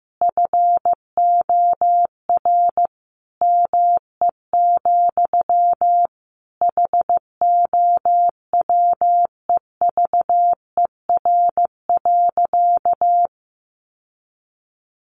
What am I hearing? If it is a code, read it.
Morse code, heard as a keyed tone, about 15 wpm, 700 Hz. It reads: FOR ME, HOWEVER.
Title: Sentences from Top 600 Words 15wpm